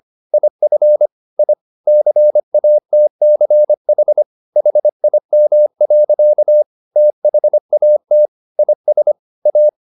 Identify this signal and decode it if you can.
Morse code, heard as a keyed tone, about 25 wpm, 600 Hz. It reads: IF I CATCH HIM. THAT IS A
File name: the old man and the sea part 1 25wpm